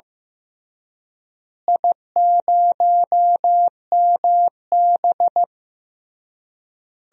Morse code, heard as a keyed tone, about 16 wpm, 700 Hz. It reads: I0MB
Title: DXCC-100 Call Signs - 4 Characters - Repeated 3x - 15wpm